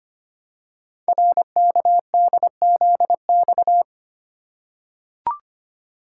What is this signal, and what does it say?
Morse code, heard as a keyed tone, about 25 wpm, 700 Hz. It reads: RKDZX
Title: Code Groups - 5 Letters - 25wpm